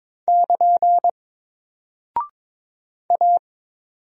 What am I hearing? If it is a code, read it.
Morse code, heard as a keyed tone, about 22 wpm, 700 Hz. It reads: <KN> A
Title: Single Character (Letter:Number:Punctuation:Prosign) Alt 22wpm